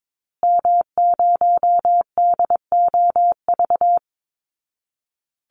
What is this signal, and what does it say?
Morse code, heard as a keyed tone, about 22 wpm, 700 Hz. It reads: M0DOV
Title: Call Signs Repeated 2x - 22wpm